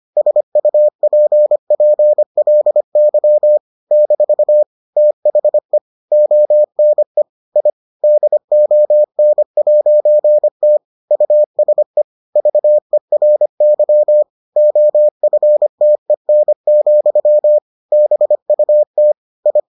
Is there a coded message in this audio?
Yes: Morse code from 0.2 to 19.6 s, heard as a keyed tone, about 25 wpm, 600 Hz. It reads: SUPPLY - THE ONE I DON'T USE VERY OFTEN, BUT I